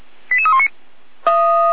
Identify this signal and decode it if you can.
Five-tone selective call (ZVEI), heard as quick runs of five tones, tones of about 70 ms. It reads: 88319
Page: SDR/ZVEI